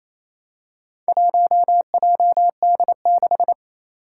Morse code, heard as a keyed tone, about 29 wpm, 700 Hz. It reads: 1JD6
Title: Four Challenging Letter-Number Combinations 28wpm